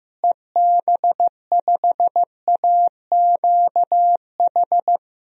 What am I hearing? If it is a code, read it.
Morse code, heard as a keyed tone, about 15 wpm, 700 Hz. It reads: EB5AQH
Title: DXCC-100 Call Signs - 6 Characters - Repeated 3x - 15wpm